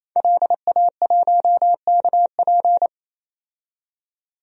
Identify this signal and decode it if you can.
Morse code, heard as a keyed tone, about 28 wpm, 700 Hz. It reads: LA1KP
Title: DXCC-100 Call Signs - 5 Characters - 28wpm